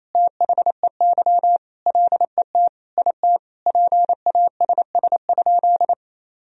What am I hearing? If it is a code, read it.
Morse code, heard as a keyed tone, about 27 wpm, 700 Hz. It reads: THEY LET IT PASS?
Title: Sentences from Top 400 Words and Abbreviations 28wpm